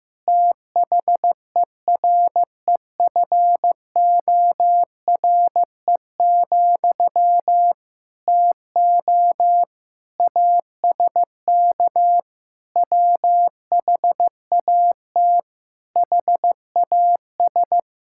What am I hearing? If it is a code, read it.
Morse code, heard as a keyed tone, about 14 wpm, 700 Hz. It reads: THEREFORE, TO ASK WHAT HAS
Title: Sentences from Top 1000 Words - Review Part 5 - 15wpm